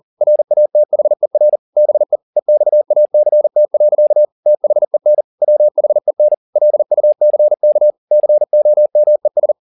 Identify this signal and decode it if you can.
Morse code, heard as a keyed tone, about 40 wpm, 600 Hz. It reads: RATHER BE EXACT. THEN WHEN LUCK COMES